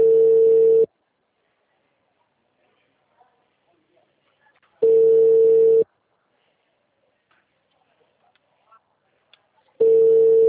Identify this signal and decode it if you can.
Ringback tone (China, 450 Hz).
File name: out- -2002- - - .66105